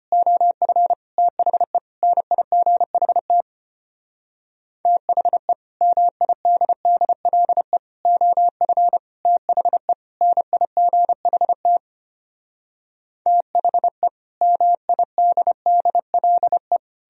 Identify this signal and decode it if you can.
Morse code, heard as a keyed tone, about 33 wpm, 700 Hz. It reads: OF THE NIGHT THE MIDDLE OF THE NIGHT THE MIDDLE